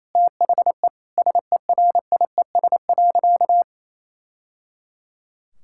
Morse code, heard as a keyed tone, about 27 wpm, 700 Hz. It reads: THE SERIES.
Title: Sentences from Top 1000 Words - Review Part 5 - 28wpm